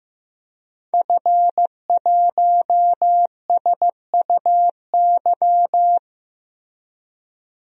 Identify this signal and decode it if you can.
Morse code, heard as a keyed tone, about 15 wpm, 700 Hz. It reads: F1SUY
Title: DXCC-100 Call Signs - 5 Characters - Repeated 2x - 15wpm